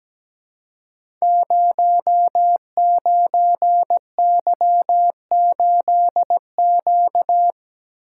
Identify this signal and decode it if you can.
Morse code, heard as a keyed tone, about 17 wpm, 700 Hz. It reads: 09Y8Q
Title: Code Groups - 5 Letter:Numbers - 17wpm